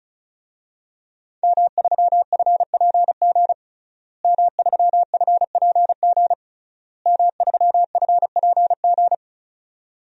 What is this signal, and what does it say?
Morse code, heard as a keyed tone, about 36 wpm, 700 Hz. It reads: M3FPG M3FPG M3FPG
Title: DXCC-100 Call Signs - 5 Characters - Repeated 3x - 35wpm